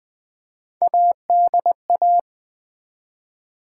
Morse code, heard as a keyed tone, about 20 wpm, 700 Hz. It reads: ADA
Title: Build A Word - 1 Letter At A Time - 5-Letter Words - 20wpm